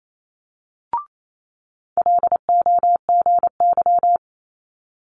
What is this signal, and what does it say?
Morse code, heard as a keyed tone, about 28 wpm, 700 Hz. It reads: LOGY
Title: Multi-letter Phonemes Quick Fire 28wpm